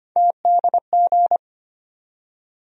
Morse code, heard as a keyed tone, about 25 wpm, 700 Hz. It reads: TDG
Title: USA Call Sign Postfixes - 3 Letters 25wpm